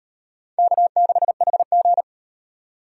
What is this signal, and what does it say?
Morse code, heard as a keyed tone, about 35 wpm, 700 Hz. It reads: K6HG